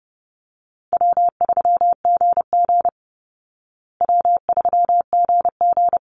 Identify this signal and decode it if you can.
Morse code, heard as a keyed tone, about 30 wpm, 700 Hz. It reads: W3GG W3GG
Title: Call Signs Repeated 3x - 30wpm